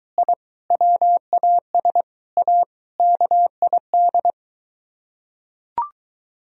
Morse code, heard as a keyed tone, about 22 wpm, 700 Hz. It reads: I WAS A KID